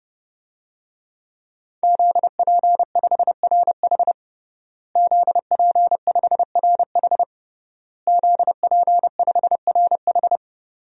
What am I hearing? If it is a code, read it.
Morse code, heard as a keyed tone, about 32 wpm, 700 Hz. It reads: ZP5RH ZP5RH ZP5RH
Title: DXCC-100 Call Signs - 5 Characters - Repeated 3x - 30wpm